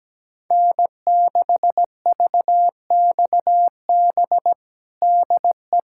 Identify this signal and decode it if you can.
Morse code, heard as a keyed tone, about 17 wpm, 700 Hz. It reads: N6VXB DE
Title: USA Callsign DE Callsign - 5 Characters - 17wpm